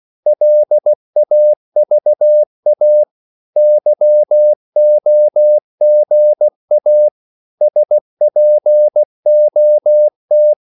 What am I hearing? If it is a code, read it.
Morse code, heard as a keyed tone, about 16 wpm, 600 Hz. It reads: LAVA YOGA SPOT
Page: Korte danske ord 16wpm